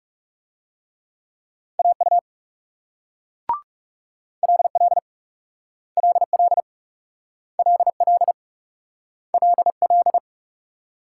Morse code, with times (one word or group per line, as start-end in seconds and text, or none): 1.8-2.2 AU
4.4-5.0 LL
6.0-6.6 LL
7.6-8.3 LL
9.3-10.2 LL